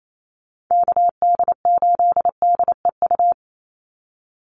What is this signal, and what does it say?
Morse code, heard as a keyed tone, about 28 wpm, 700 Hz. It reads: KD8DEU
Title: NA SOTA Chaser Callsigns 28wpm